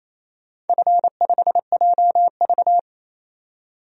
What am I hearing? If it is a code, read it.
Morse code, heard as a keyed tone, about 28 wpm, 700 Hz. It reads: F5JV
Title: DXCC-100 Call Signs - 4 Characters - 28wpm